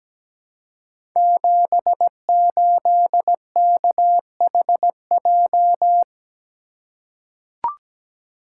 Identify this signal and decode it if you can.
Morse code, heard as a keyed tone, about 17 wpm, 700 Hz. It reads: 78KHJ
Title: Code Groups - 5 Letter:Numbers - 17wpm